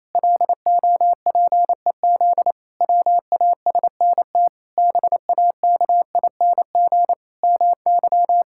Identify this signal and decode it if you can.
Morse code, heard as a keyed tone, about 28 wpm, 700 Hz. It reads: LOPEZ WASNT BAKING MY
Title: Pangram Challenge v2 - 28wpm